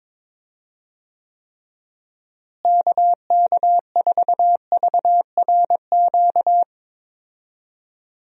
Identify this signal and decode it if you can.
Morse code, heard as a keyed tone, about 22 wpm, 700 Hz. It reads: KK4VRQ
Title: USA Call Signs - 6 Characters - Repeated 3x - 22wpm